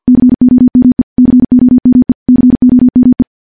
Morse code, heard as a keyed tone, about 49 wpm, 250 Hz. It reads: COME COME COME
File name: morse / morse-training-50WPM